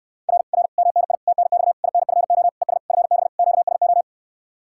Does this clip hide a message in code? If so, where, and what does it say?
Yes: Morse code from 0.3 to 4.0 s, heard as a keyed tone, about 69 wpm, 700 Hz. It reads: R R MNI TNX INFO ES FB QSO